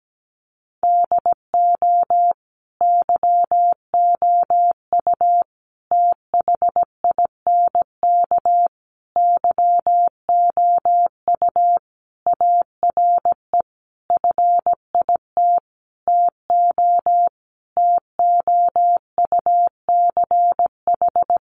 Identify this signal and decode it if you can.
Morse code, heard as a keyed tone, about 17 wpm, 700 Hz. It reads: DO YOU THINK YOU ARE FIT TO TOUCH
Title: Sentences from Top 1200 Words 17wpm